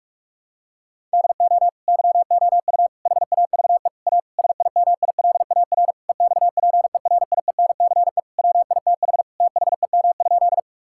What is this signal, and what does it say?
Morse code, heard as a keyed tone, about 46 wpm, 700 Hz. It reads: DO YOU HAVE A SIMILAR EXPERIENCE WITH THEM?
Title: Sentences from Top 1000 Words - Review Part 5 - 45wpm